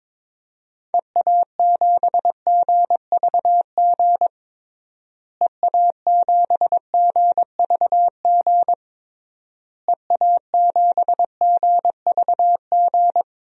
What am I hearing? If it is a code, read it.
Morse code, heard as a keyed tone, about 22 wpm, 700 Hz. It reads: EA7GVG EA7GVG EA7GVG
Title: DXCC-100 Call Signs - 6 Characters - Repeated 3x - 22wpm